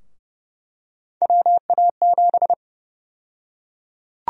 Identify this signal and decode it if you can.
Morse code, heard as a keyed tone, about 31 wpm, 700 Hz. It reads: WA7
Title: USA Call Sign Prefixes - Repeated 2x - 30wpm